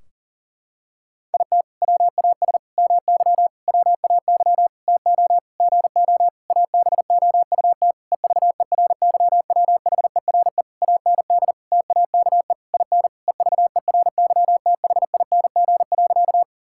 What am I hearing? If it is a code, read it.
Morse code, heard as a keyed tone, about 37 wpm, 700 Hz. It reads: IT WAS MY WAY TO GO ABOUT EVERYWHERE AND TAKE IN EVERYTHING.
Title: Sentences from Top 1600 Words 40wpm